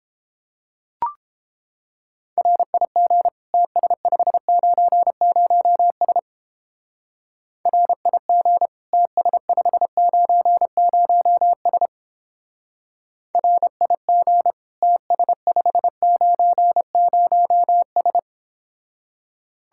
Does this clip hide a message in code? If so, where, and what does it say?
Yes: Morse code from 2.4 to 18.2 s, heard as a keyed tone, about 31 wpm, 700 Hz. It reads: RIG TS590S RIG TS590S RIG TS590S